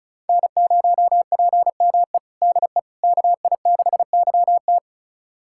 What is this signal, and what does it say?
Morse code, heard as a keyed tone, about 36 wpm, 690 Hz. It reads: N0PME DE KI6YT
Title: USA Callsign DE Callsign - 5 Characters - 35wpm